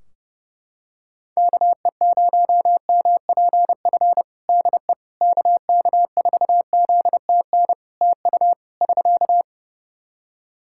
Morse code, heard as a keyed tone, about 30 wpm, 700 Hz. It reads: KE0MPF DE KK4ZTN TU <SK>
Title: Practice QSO - Exchange 4 - Ending - 30wpm